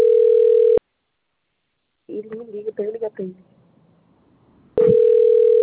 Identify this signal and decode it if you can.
Ringback tone (North America), 440 + 480 Hz, with speech.
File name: out- - - - - .37120